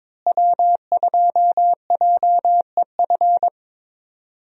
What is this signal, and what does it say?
Morse code, heard as a keyed tone, about 21 wpm, 700 Hz. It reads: W2JEF
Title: USA Call Signs - 5 Characters - Repeated 2x - 22wpm